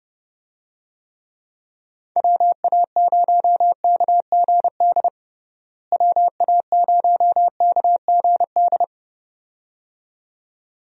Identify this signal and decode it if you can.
Morse code, heard as a keyed tone, about 30 wpm, 700 Hz. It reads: WA0KGD WA0KGD
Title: USA Call Signs - 6 Characters - Repeated 2x - 30wpm